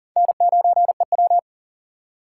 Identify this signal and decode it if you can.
Morse code, heard as a keyed tone, about 40 wpm, 700 Hz. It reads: N9EW
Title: USA Call Signs - 4 Characters - 4x Character Spacing - 40wpm